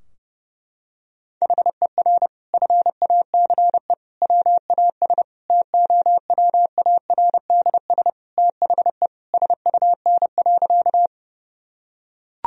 Morse code, heard as a keyed tone, about 30 wpm, 700 Hz. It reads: HER FACE WAS TOWARDS THE SUN.
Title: Sentences from Top 1000 Words - Review Part 2 - 30wpm